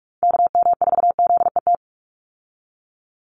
Morse code, heard as a keyed tone, about 44 wpm, 700 Hz. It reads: KM4ZET
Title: USA Call Signs 45wpm